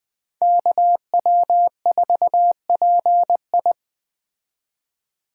Morse code, heard as a keyed tone, about 20 wpm, 700 Hz. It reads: KW4PI
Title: Speed Racing - USA Call Signs 17wpm